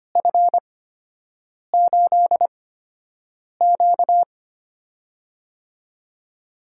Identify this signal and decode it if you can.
Morse code, heard as a keyed tone, about 25 wpm, spaced as for 7 wpm, 700 Hz. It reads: F8Q
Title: USA Call Signs - 4 Characters - 8x Character Spacing - 25wpm